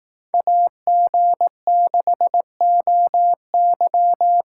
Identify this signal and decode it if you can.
Morse code, heard as a keyed tone, about 18 wpm, 700 Hz. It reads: AG6OY
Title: Speed Racing - USA 2x2 Call Signs 15wpm